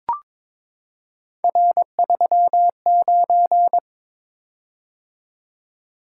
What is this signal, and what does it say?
Morse code, heard as a keyed tone, about 23 wpm, 700 Hz. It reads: R39
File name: practice / 22wpm / Three Letter-Number Combinations - 22wpm